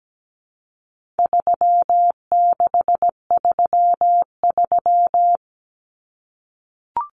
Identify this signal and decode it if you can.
Morse code, heard as a keyed tone, about 17 wpm, 700 Hz. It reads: 3633
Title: Common Cut Numbers - 4-Digit 17wpm